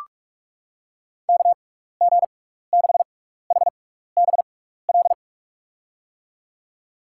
Morse code, heard as a keyed tone, about 46 wpm, spaced as for 15 wpm, 700 Hz. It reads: KG6HBL
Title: USA Call Signs - 6x Character Spacing - 45wpm